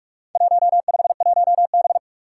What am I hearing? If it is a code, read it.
Morse code, heard as a keyed tone, about 41 wpm, 700 Hz. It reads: 151B
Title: Four Challenging Letter-Number Combinations 45wpm